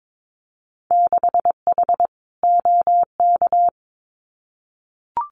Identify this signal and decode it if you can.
Morse code, heard as a keyed tone, about 22 wpm, 700 Hz. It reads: 6H OK
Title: Winter Field Day Exchanges Encore 22wpm